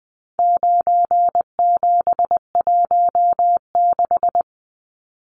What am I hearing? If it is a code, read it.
Morse code, heard as a keyed tone, about 20 wpm, 700 Hz. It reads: 9716
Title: Numbers - 4-Digit 20wpm